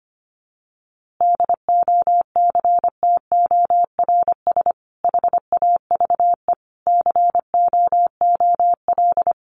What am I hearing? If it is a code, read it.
Morse code, heard as a keyed tone, about 25 wpm, 700 Hz. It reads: DOCTORS HAVE COOL
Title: Easy Phrases 25wpm